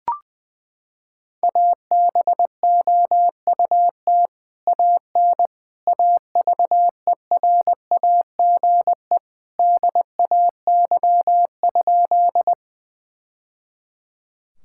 Morse code, heard as a keyed tone, about 20 wpm, 700 Hz. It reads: ABOUT AN AVERAGE DAY?
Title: Sentences from Top 1000 Words - Review Part 6 - 20wpm